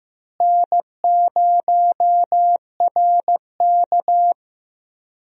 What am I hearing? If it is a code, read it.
Morse code, heard as a keyed tone, about 15 wpm, 700 Hz. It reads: N0RK
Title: USA Call Signs - 4 Characters - Repeated 2x - 15wpm